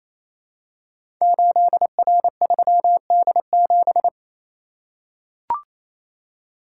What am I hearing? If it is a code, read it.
Morse code, heard as a keyed tone, about 27 wpm, 700 Hz. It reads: 8R3D7
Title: Code Groups - 5 Letter:Numbers - 28wpm